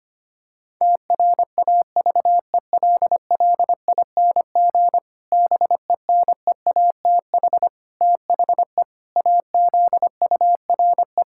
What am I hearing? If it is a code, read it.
Morse code, heard as a keyed tone, about 25 wpm, 700 Hz. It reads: TRAVELLING BENEATH THE AZURE
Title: Pangram Challenge v2 - 25wpm